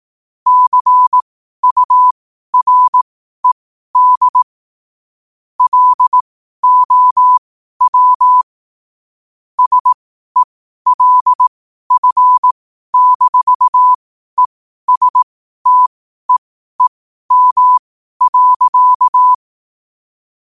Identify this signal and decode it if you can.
Morse code, heard as a keyed tone, about 18 wpm, spaced as for 12 wpm, 1000 Hz. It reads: CURED LOW SELF-ESTEEM.
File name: qotdfiles / 12wpm